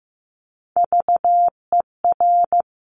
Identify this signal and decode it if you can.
Morse code, heard as a keyed tone, about 15 wpm, 700 Hz. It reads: VER
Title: Top Three Letter Combinations in Words 15wpm